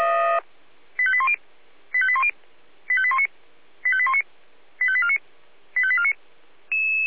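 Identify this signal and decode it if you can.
Five-tone selective call (ZVEI), heard as quick runs of five tones, tones of about 70 ms. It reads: 85810 85810 85819 85819 85849 85849
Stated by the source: ZVEI five-tone selective call received by SDR.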